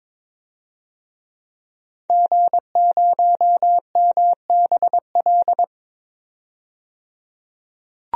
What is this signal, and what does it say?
Morse code, heard as a keyed tone, about 22 wpm, 700 Hz. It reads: G0MBL
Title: DXCC-100 Call Signs - 5 Characters - 22wpm